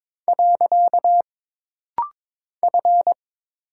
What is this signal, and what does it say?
Morse code, heard as a keyed tone, about 22 wpm, 700 Hz. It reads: . F
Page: This rapid-fire timing is geared to push your ICR (Instant Character Recognition) skills.